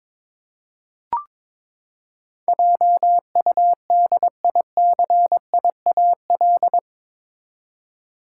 Morse code, heard as a keyed tone, about 22 wpm, 700 Hz. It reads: JUDICIAL
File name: practice / 22wpm / Eight Letter Words - 22wpm